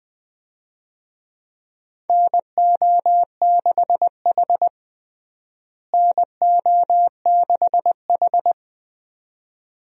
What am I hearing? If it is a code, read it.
Morse code, heard as a keyed tone, about 20 wpm, 700 Hz. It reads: NO6H NO6H
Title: USA Call Signs - 4 Characters - Repeated 2x - 20wpm